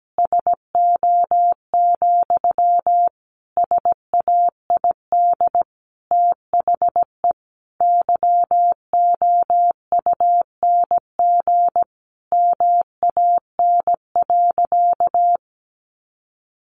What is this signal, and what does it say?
Morse code, heard as a keyed tone, about 17 wpm, 700 Hz. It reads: SO, SAID THE YOUNG MAN.
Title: Sentences from Top 900 Words 17wpm